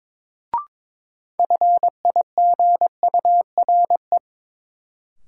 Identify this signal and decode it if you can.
Morse code, heard as a keyed tone, about 22 wpm, 700 Hz. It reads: FIGURE
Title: Sets of 4 Words - New Word Drill - 22wpm